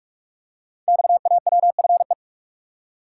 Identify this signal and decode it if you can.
Morse code, heard as a keyed tone, about 44 wpm, 700 Hz. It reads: XAWFE
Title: Code Groups - 5 Letters - 45wpm